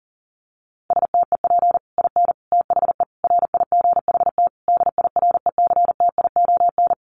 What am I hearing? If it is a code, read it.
Morse code, heard as a keyed tone, about 40 wpm, 700 Hz. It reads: STEP IN THE RIGHT DIRECTION
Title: Sets of 5 Words Encore 40wpm